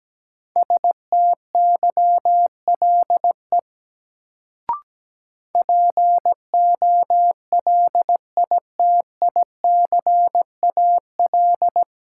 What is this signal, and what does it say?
Morse code, heard as a keyed tone, about 17 wpm, 700 Hz. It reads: STYLE POLITICAL
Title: Sentences from Top 800 Words - New Word Drill - 17wpm